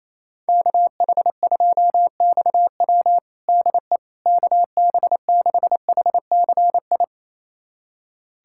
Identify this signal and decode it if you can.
Morse code, heard as a keyed tone, about 28 wpm, 700 Hz. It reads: KH2XW DE KB6HCI
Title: USA Callsign DE Callsign 28wpm